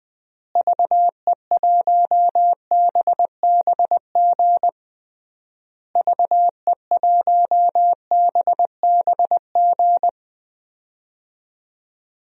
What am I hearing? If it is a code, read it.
Morse code, heard as a keyed tone, about 20 wpm, 700 Hz. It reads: VE1BBG VE1BBG
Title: DXCC-100 Call Signs - 6 Characters - Repeated 3x - 20wpm